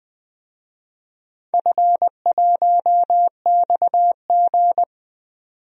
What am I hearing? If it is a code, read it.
Morse code, heard as a keyed tone, about 20 wpm, 700 Hz. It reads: F1XG
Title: DXCC-100 Call Signs - 4 Characters - 20wpm